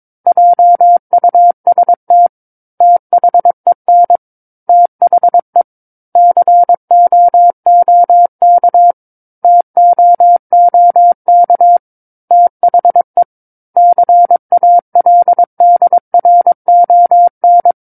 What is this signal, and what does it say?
Morse code, heard as a keyed tone, about 23 wpm, 700 Hz. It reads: JUST THEN THE COOK TOOK THE CALDRON
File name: 06-Chapter-06 22 WPM